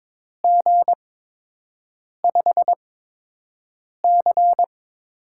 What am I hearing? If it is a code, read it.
Morse code, heard as a keyed tone, about 21 wpm, spaced as for 6 wpm, 700 Hz. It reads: G5C
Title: USA Call Signs - 8x Character Spacing - 22wpm